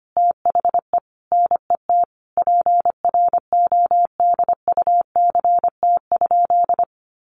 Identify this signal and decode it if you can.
Morse code, heard as a keyed tone, about 25 wpm, 700 Hz. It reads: THE NET PRODUCT?
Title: Sentences from Top 1400 Words 25wpm